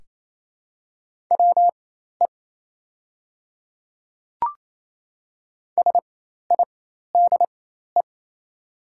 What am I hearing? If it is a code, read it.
Morse code, heard as a keyed tone, about 28 wpm, spaced as for 13 wpm, 690 Hz. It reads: WE SIDE
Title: 4x Character Spacing - Top 300 Words - 28wpm